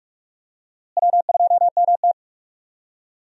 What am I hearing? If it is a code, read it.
Morse code, heard as a keyed tone, about 43 wpm, 700 Hz. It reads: W2MT
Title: USA Call Signs - 4 Characters - Repeated 2x - 45wpm